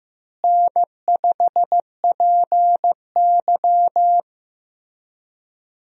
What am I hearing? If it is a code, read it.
Morse code, heard as a keyed tone, about 15 wpm, 700 Hz. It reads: N5PY
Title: USA Call Signs - 4 Characters - Repeated 3x - 15wpm